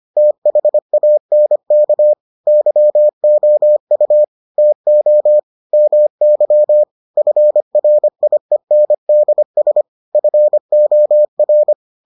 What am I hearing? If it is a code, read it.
Morse code, heard as a keyed tone, about 25 wpm, 600 Hz. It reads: THANK YOU TO MY FRIENDS FOR